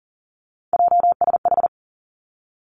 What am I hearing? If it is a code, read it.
Morse code, heard as a keyed tone, about 42 wpm, 700 Hz. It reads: JSH
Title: Three Challenging Letter-Number Combinations 40wpm